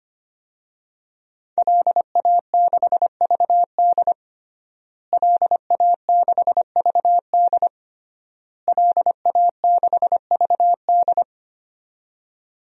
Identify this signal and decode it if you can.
Morse code, heard as a keyed tone, about 25 wpm, 700 Hz. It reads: LA6VD LA6VD LA6VD
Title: DXCC-100 Call Signs - 5 Characters - Repeated 3x - 25wpm